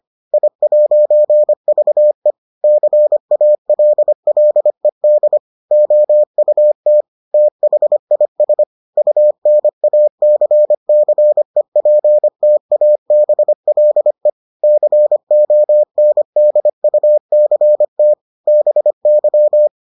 Morse code, heard as a keyed tone, about 25 wpm, 600 Hz. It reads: I'VE CALLED OUT THIS UNACCEPTABLE CONDUCT BY